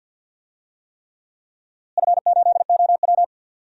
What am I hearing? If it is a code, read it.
Morse code, heard as a keyed tone, about 48 wpm, 700 Hz. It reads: F8OW
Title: DXCC-100 Call Signs - 4 Characters - Repeated 3x - 50wpm